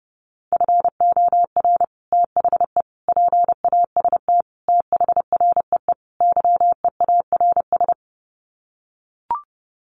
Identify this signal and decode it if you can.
Morse code, heard as a keyed tone, about 30 wpm, 700 Hz. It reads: FOR THE PAST THREE YEARS
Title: Sets of 5 Words Encore2 30wpm